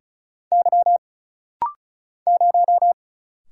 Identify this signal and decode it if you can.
Morse code, heard as a keyed tone, about 36 wpm, 700 Hz. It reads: Y 0
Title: Single Letter-Number - Rapid-Fire 35wpm